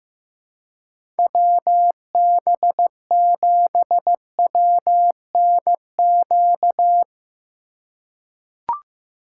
Morse code, heard as a keyed tone, about 15 wpm, 700 Hz. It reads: WB7WNQ
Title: USA Call Signs - 6 Characters - 2x Character Spacing - 15wpm